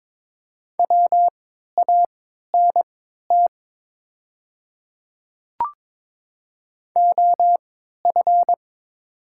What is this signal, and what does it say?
Morse code, heard as a keyed tone, about 22 wpm, spaced as for 13 wpm, 700 Hz. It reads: WANT OF
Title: 3x Character Spacing - Top 300 Words - 22wpm